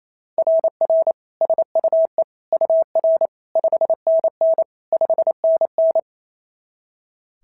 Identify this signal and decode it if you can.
Morse code, heard as a keyed tone, about 27 wpm, 650 Hz. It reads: RR SUE UR 5NN 5NN
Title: SOTA QSO Practice 28wpm